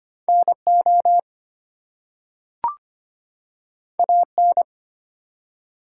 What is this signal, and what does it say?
Morse code, heard as a keyed tone, about 25 wpm, 700 Hz. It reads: NO AN
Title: Two Letter Words - 25wpm